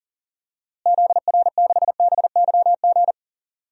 Morse code, heard as a keyed tone, about 40 wpm, 700 Hz. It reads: ZR6BYG
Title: DXCC-100 Call Signs - 6 Characters - Repeated 2x - 40wpm